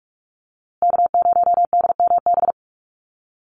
Morse code, heard as a keyed tone, about 44 wpm, 700 Hz. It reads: K0DMB
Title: USA Call Signs Repeated 2x 45wpm